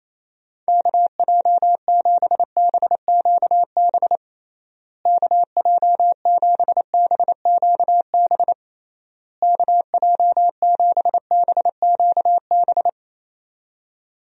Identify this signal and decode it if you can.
Morse code, heard as a keyed tone, about 28 wpm, 700 Hz. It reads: KJ7BQB KJ7BQB KJ7BQB
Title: USA Call Signs - 6 Characters - Repeated 3x - 28wpm